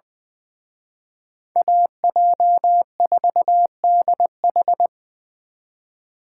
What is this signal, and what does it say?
Morse code, heard as a keyed tone, about 23 wpm, 700 Hz. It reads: AJ4DH
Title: Speed Racing - USA 2x2 Call Signs 20wpm